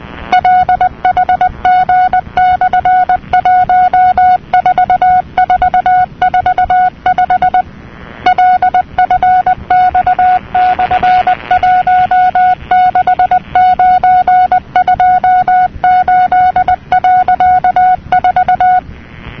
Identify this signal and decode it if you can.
Morse code, heard as a keyed tone, about 20 wpm, 730 Hz. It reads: LHG/14445 LFX/16928.4
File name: europe / nor / nor-237-ptt jeloey'lcj,...'-cw-(1985)-(rb)-(mc-909-2-10)